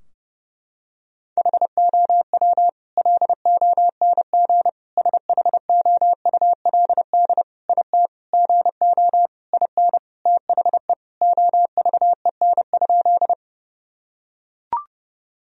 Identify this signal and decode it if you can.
Morse code, heard as a keyed tone, about 30 wpm, 700 Hz. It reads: HOW LONG SHOULD IT GO IN THE OVEN?
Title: Sentences from Top 1000 Words - Review Part 4 - 30wpm